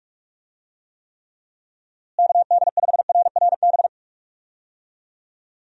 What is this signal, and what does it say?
Morse code, heard as a keyed tone, about 45 wpm, 700 Hz. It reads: KD5RRB
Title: USA Call Signs - 6 Characters - Repeated 2x - 45wpm